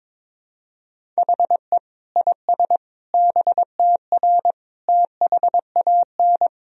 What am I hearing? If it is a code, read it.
Morse code, heard as a keyed tone, about 22 wpm, 700 Hz. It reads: HE IS BTR THAN
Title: Sentences from Top 400 Words and Abbreviations 22wpm